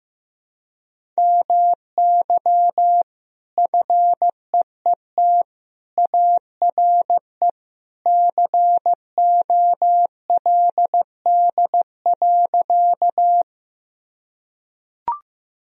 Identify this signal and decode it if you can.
Morse code, heard as a keyed tone, about 16 wpm, 700 Hz. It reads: MY FEET ARE COLD.
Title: Sentences from Top 1000 Words - Review Part 1 - 15wpm